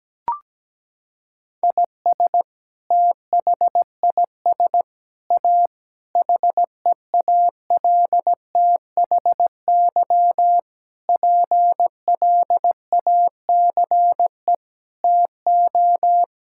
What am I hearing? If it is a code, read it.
Morse code, heard as a keyed tone, about 17 wpm, 700 Hz. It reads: IS THIS A HEALTHY PLACE TO
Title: Sentences from Top 1000 Words - Review Part 5 - 17wpm